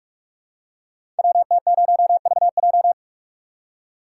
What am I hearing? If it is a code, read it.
Morse code, heard as a keyed tone, about 44 wpm, 700 Hz. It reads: WT0VJ
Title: Code Groups - 5 Letter:Numbers - 45wpm